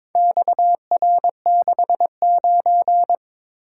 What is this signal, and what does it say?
Morse code, heard as a keyed tone, about 22 wpm, 700 Hz. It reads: XR69
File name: practice / 22wpm / Four Letter-Number Combinations - 22wpm